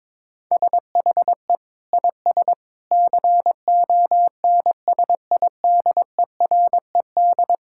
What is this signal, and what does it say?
Morse code, heard as a keyed tone, about 22 wpm, 700 Hz. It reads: SHE IS CONSIDERED
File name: practice / 22wpm / Sentences with 8-Word Phrases - 22wpm